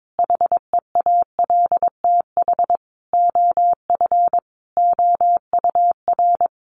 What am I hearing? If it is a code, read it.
Morse code, heard as a keyed tone, about 22 wpm, 700 Hz. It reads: HEALTH OF OUR
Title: Sentences from Top 1500 Words 22wpm